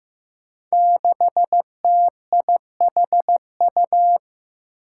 Morse code, heard as a keyed tone, about 15 wpm, 700 Hz. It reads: 6TIHU
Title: Five Challenging Letter-Number Combinations 15wpm